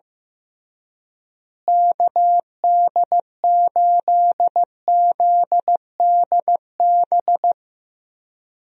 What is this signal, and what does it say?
Morse code, heard as a keyed tone, about 15 wpm, 700 Hz. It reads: KD8ZDB
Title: USA Call Signs - 6 Characters - Repeated 3x - 15wpm